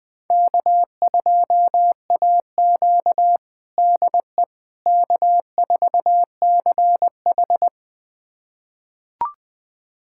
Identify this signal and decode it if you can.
Morse code, heard as a keyed tone, about 21 wpm, 700 Hz. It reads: K2AQ DE K4CH
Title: USA Callsign DE Callsign - 4 Characters - 20wpm